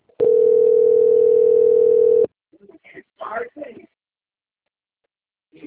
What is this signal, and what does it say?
Ringback tone (North America), 440 + 480 Hz, with speech.
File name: rg-1600-unknown- - - .33317